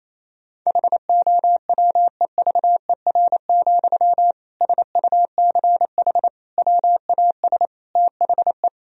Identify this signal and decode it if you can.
Morse code, heard as a keyed tone, about 27 wpm, 700 Hz. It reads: HOWEVER, SUCH WAS THE
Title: Sentences from Top 1000 Words - Review Part 3 - 28wpm